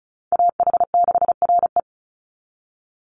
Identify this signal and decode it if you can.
Morse code, heard as a keyed tone, about 34 wpm, 700 Hz. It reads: AH6RE
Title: Call Signs - 35wpm